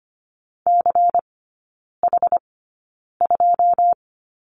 Morse code, heard as a keyed tone, about 25 wpm, spaced as for 9 wpm, 700 Hz. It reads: CH2
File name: course / 25:10wpm / 093 Lesson 16e - Letter c - Three Characters 25wpm